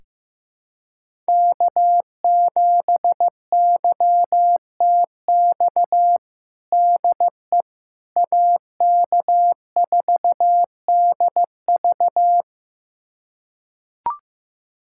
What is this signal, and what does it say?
Morse code, heard as a keyed tone, about 15 wpm, 700 Hz. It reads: K7YTX DE AK4DV
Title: USA Callsign DE Callsign - 5 Characters - 15wpm